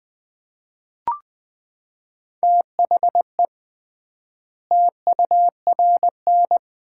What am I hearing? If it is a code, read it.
Morse code, heard as a keyed tone, about 21 wpm, 700 Hz. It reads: THE TURN
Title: 3x Word-Spacing - Sets of 4 Words - 20wpm